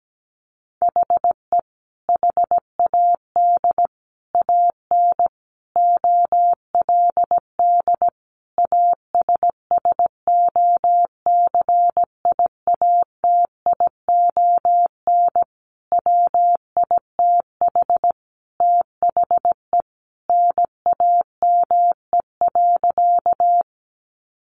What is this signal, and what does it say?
Morse code, heard as a keyed tone, about 17 wpm, 700 Hz. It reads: HE HAD AN OLD ASSOCIATION WITH THE NAME.
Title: Sentences from Top 1100 Words 17wpm